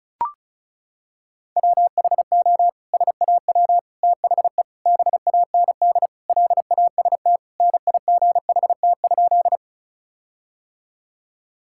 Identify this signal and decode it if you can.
Morse code, heard as a keyed tone, about 36 wpm, 700 Hz. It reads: WHO SAW THE BAND LAST NIGHT?
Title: Sentences from Top 1600 Words 35wpm